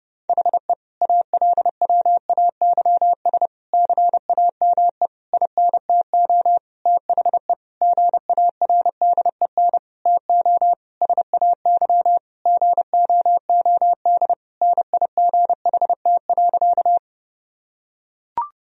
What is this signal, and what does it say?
Morse code, heard as a keyed tone, about 30 wpm, 700 Hz. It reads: HE ALWAYS CAME INTO THE GARDEN TO SAY GOOD NIGHT.
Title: Sentences from Top 1000 Words - Review Part 4 - 30wpm